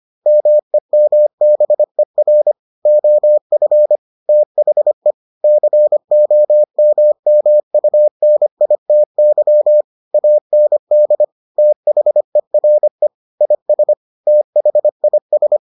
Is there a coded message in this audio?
Yes: Morse code from 0.3 to 15.6 s, heard as a keyed tone, about 25 wpm, 600 Hz. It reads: MEMBER OF THE COMMUNITY AND THERE IS THIS